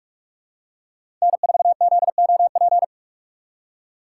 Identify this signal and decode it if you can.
Morse code, heard as a keyed tone, about 46 wpm, 700 Hz. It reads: N4ZOP
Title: USA Call Signs - 5 Characters - 2x Character Spacing - 45wpm